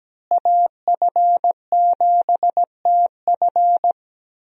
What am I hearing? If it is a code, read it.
Morse code, heard as a keyed tone, about 17 wpm, 700 Hz. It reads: AF7TF
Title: USA Call Signs - 5 Characters - Repeated 2x - 17wpm